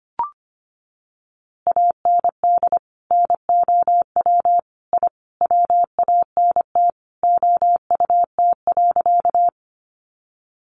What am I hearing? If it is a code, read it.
Morse code, heard as a keyed tone, about 25 wpm, 700 Hz. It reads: AND NOW I WANT OUT.
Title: A Sound Of Thunder 25wpm